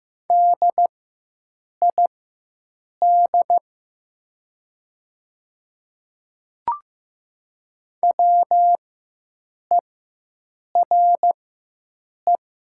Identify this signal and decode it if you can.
Morse code, heard as a keyed tone, about 15 wpm, spaced as for 7 wpm, 700 Hz. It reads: DID WERE
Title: 4x Character Spacing - Top 100 Words - 15wpm